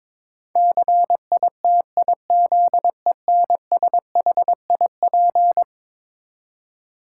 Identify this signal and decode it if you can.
Morse code, heard as a keyed tone, about 22 wpm, 700 Hz. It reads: CITIZENSHIP
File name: practice / 22wpm / Eleven Letter Words - 22wpm